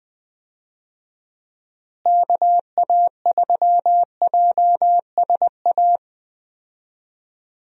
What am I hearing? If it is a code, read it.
Morse code, heard as a keyed tone, about 20 wpm, 700 Hz. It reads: KA3JSA
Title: USA Call Signs - 6 Characters - Repeated 3x - 20wpm